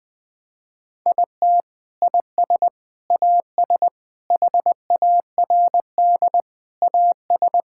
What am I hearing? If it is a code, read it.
Morse code, heard as a keyed tone, about 20 wpm, 700 Hz. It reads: IT IS AS HARD AS
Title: Sentences from Top 1600 Words 20wpm